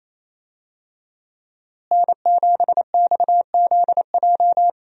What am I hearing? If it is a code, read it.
Morse code, heard as a keyed tone, about 28 wpm, 700 Hz. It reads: N7XZJ
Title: USA Call Signs - 5 Characters - Repeated 3x - 28wpm